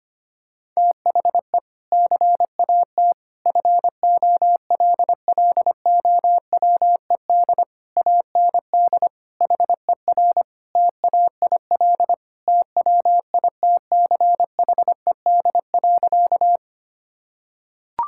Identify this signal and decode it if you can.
Morse code, heard as a keyed tone, about 25 wpm, 700 Hz. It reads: THE CAT FOLLOWED AND HER TAIL TWITCHED.
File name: practice / 25wpm / Sentences with 2-Word Phrases - 25wpm